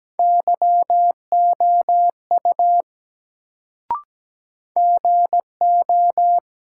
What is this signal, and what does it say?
Morse code, heard as a keyed tone, about 17 wpm, 700 Hz. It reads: YOU GO
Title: Sentences from Top 100 Words - New Word Drill - 17wpm